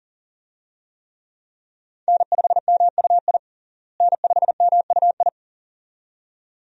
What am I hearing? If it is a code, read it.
Morse code, heard as a keyed tone, about 37 wpm, 700 Hz. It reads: N5MUI N5MUI
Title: USA Call Signs - 5 Characters - Repeated 2x - 40wpm